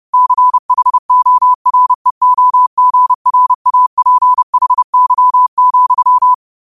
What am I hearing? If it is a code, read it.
Morse code, heard as a keyed tone, about 30 wpm, 1000 Hz. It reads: CHOREOGRAPHY,
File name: qotdfiles / 30wpm